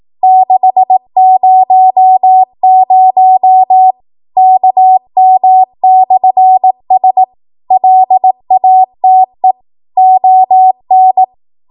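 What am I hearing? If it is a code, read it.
Morse code, heard as a keyed tone, about 18 wpm, 750 Hz. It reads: 600 KM/S LATE ON
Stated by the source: ARRL 18 wpm Morse code practice archive transmission.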